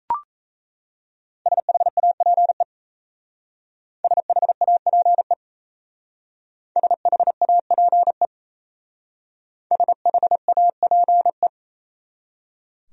Morse code, heard as a keyed tone, about 35 wpm, 690 Hz. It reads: SHAPE SHAPE SHAPE SHAPE
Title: Speed-Racing Top 500 Words 28wpm